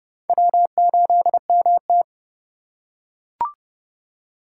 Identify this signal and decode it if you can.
Morse code, heard as a keyed tone, about 29 wpm, 700 Hz. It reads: W8MT